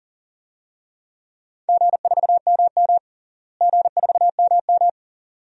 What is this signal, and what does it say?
Morse code, heard as a keyed tone, about 40 wpm, 700 Hz. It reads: G4MM G4MM
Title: DXCC-100 Call Signs - 4 Characters - Repeated 2x - 40wpm